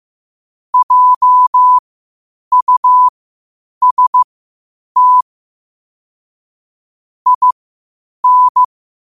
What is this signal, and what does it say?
Morse code, heard as a keyed tone, about 15 wpm, spaced as for 8 wpm, 1000 Hz. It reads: JUST IN
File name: qotdfiles / 8wpm